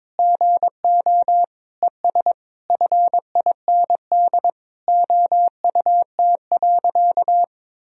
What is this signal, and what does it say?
Morse code, heard as a keyed tone, about 22 wpm, 700 Hz. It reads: GO ES FIND OUT.
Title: Sentences from Top 200 Words and Abbreviations 22wpm